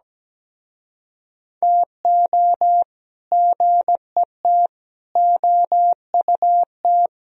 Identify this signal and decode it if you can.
Morse code, heard as a keyed tone, about 18 wpm, 700 Hz. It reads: TO GET OUT
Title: Speed-Racing - Sets of 4 Words 17wpm